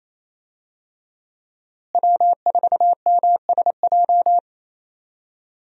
Morse code, heard as a keyed tone, about 27 wpm, 700 Hz. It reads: W4MSJ
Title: USA Call Signs - 5 Characters - Repeated 2x - 28wpm